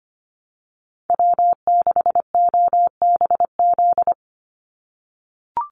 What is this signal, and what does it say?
Morse code, heard as a keyed tone, about 25 wpm, 700 Hz. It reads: W6OBZ
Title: USA Call Signs 25wpm